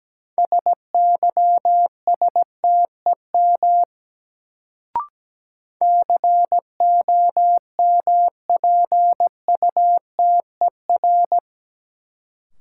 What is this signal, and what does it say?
Morse code, heard as a keyed tone, about 17 wpm, 700 Hz. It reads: SYSTEM COMPUTER
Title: Sentences from Top 1000 Words - Review Part 3 - New Word Drill - 17wpm